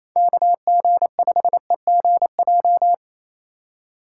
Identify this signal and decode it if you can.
Morse code, heard as a keyed tone, about 28 wpm, 700 Hz. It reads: KG5EGJ
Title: USA Call Signs - 6 Characters - 4x Character Spacing - 28wpm